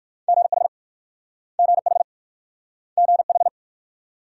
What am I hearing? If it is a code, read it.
Morse code, heard as a keyed tone, about 54 wpm, 700 Hz. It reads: GH GH GH
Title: Speed-Racing Multi-letter Phonemes - 45wpm